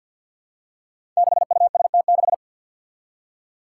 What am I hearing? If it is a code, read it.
Morse code, heard as a keyed tone, about 51 wpm, 700 Hz. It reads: 6UST6
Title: Five Challenging Letter-Number Combinations 50wpm